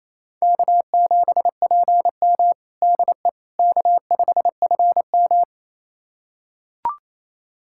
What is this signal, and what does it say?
Morse code, heard as a keyed tone, about 27 wpm, 700 Hz. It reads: K7PM DE K5FM
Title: USA Callsign DE Callsign - 4 Characters - 28wpm